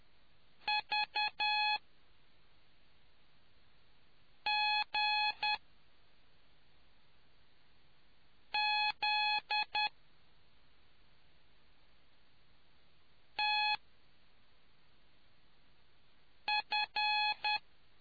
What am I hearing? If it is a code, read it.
Morse code, heard as a keyed tone, about 10 wpm, spaced as for 3 wpm, 840 Hz. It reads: VGZTF